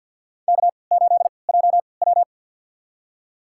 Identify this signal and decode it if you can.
Morse code, heard as a keyed tone, about 52 wpm, spaced as for 29 wpm, 700 Hz. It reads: K8JW
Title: USA Call Signs - 4 Characters - 3x Character Spacing - 50wpm